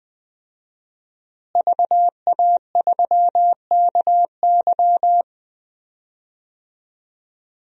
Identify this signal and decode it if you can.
Morse code, heard as a keyed tone, about 20 wpm, 700 Hz. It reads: VA3KY
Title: DXCC-100 Call Signs - 5 Characters - Repeated 3x - 20wpm